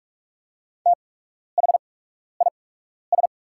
Morse code, heard as a keyed tone, about 44 wpm, spaced as for 12 wpm, 700 Hz. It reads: THIS
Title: 8x Character Spacing - Top 100 Words - 45wpm